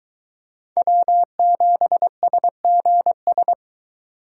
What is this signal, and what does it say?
Morse code, heard as a keyed tone, about 22 wpm, 700 Hz. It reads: W7SGS